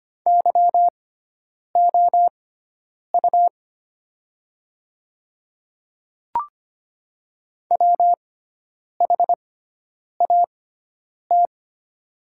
Morse code, heard as a keyed tone, about 25 wpm, spaced as for 9 wpm, 700 Hz. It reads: YOU WHAT
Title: 6x Character Spacing - Top 100 Words - 25wpm